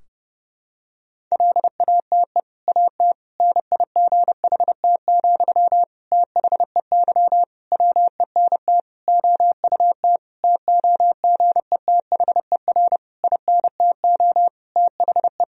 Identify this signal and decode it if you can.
Morse code, heard as a keyed tone, about 30 wpm, 700 Hz. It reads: LATE AT NIGHT, THEY WENT OUT TOGETHER INTO THE
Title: Sentences from Top 1000 Words - Review Part 3 - 30wpm